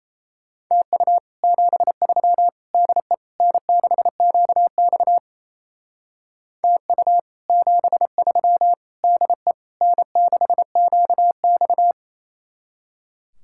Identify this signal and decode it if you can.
Morse code, heard as a keyed tone, about 36 wpm, 700 Hz. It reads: TU 73 DE N6QX TU 73 DE N6QX